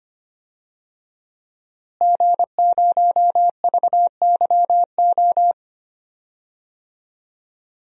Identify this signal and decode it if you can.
Morse code, heard as a keyed tone, about 25 wpm, 700 Hz. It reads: G0VYO
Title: DXCC-100 Call Signs - 5 Characters - Repeated 3x - 25wpm